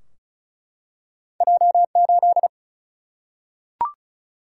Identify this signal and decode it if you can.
Morse code, heard as a keyed tone, about 36 wpm, 700 Hz. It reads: J 8
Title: Single-Letter Single-Number 35wpm